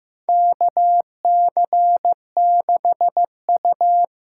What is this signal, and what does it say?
Morse code, heard as a keyed tone, about 15 wpm, 700 Hz. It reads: KC6U
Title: USA Call Signs - 6 Characters - 8x Character Spacing - 15wpm